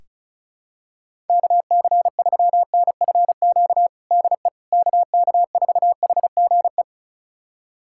Morse code, heard as a keyed tone, about 36 wpm, 700 Hz. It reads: KC3NFQ DE KK4HGE
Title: USA Callsign DE Callsign - 6 Characters - 35wpm